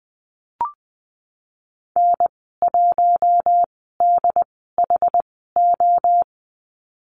Morse code, heard as a keyed tone, about 20 wpm, spaced as for 15 wpm, 700 Hz. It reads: N1DHO
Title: 088 Lesson 15g - Number 5 - Call Signs 20wpm